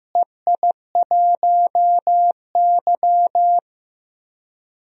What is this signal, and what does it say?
Morse code, heard as a keyed tone, about 15 wpm, 700 Hz. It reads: EI1Y
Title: DXCC-100 Call Signs - 4 Characters - Repeated 2x - 15wpm